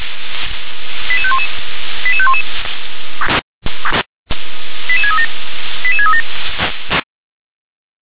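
Five-tone selective call (ZVEI), heard as quick runs of five tones, tones of about 70 ms. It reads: 88511 88511 88637 88637
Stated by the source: ZVEI five-tone call received by SDR.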